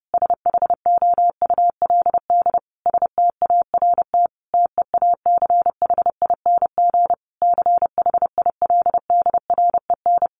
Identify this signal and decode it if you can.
Morse code, heard as a keyed tone, about 30 wpm, 700 Hz. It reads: SHOULD START TEACHING CHILDREN
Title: Sentences from Top 1100 Words 30wpm